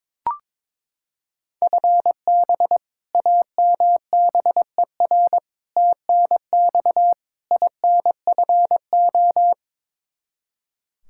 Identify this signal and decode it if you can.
Morse code, heard as a keyed tone, about 22 wpm, 700 Hz. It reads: FB AMBER TNX INFO
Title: Practice QSO - Exchange 3 - More Info - 22wpm